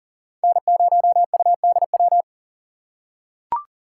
Morse code, heard as a keyed tone, about 40 wpm, 700 Hz. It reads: N0UDW
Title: USA Call Signs - 5 Characters - 2x Character Spacing - 40wpm